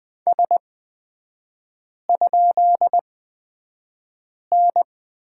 Morse code, heard as a keyed tone, about 20 wpm, spaced as for 5 wpm, 700 Hz. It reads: S?N
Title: 250 Lesson 40e - BK - Three Characters 20wpm